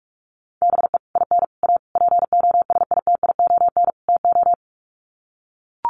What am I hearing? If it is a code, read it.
Morse code, heard as a keyed tone, about 44 wpm, 700 Hz. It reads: BE IN A POSITION TO
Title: Sets of 5 Words Encore2 45wpm